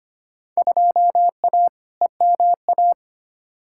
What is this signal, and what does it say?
Morse code, heard as a keyed tone, about 25 wpm, 700 Hz. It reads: 2A EMA
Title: Field Day Exchanges Encore - 25wpm